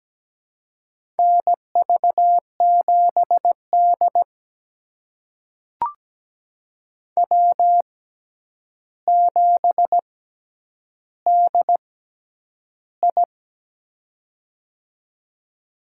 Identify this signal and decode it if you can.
Morse code, heard as a keyed tone, about 17 wpm, spaced as for 9 wpm, 700 Hz. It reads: NV7D W7DI
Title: USA Call Signs - 4 Characters - 6x Character Spacing - 17wpm